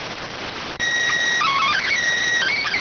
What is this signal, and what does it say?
SSTV (slow-scan television), PD 290.